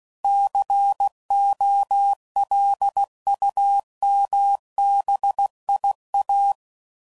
Morse code, heard as a keyed tone, about 16 wpm, 800 Hz. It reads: COLUMBIA
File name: CW Code Groups @15WPM / fromSM's / Cities